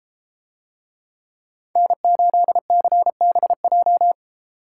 Morse code, heard as a keyed tone, about 33 wpm, 700 Hz. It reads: N8CBJ